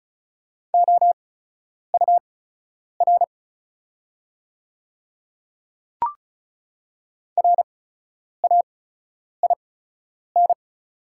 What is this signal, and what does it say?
Morse code, heard as a keyed tone, about 36 wpm, spaced as for 10 wpm, 690 Hz. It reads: OUR RAIN
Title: 8x Character Spacing - Top 300 Words - 35wpm